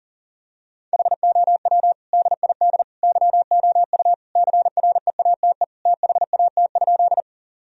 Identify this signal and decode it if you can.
Morse code, heard as a keyed tone, about 40 wpm, 700 Hz. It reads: HOW DID YOU CREATE THAT?
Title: Sentences from Top 1000 Words - Review Part 3 - 40wpm